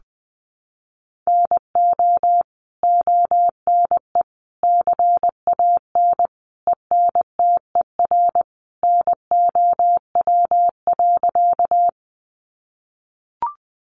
Morse code, heard as a keyed tone, about 20 wpm, 700 Hz. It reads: NO ONE CAN ENTER NOW.
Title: Sentences from Top 1300 Words 20wpm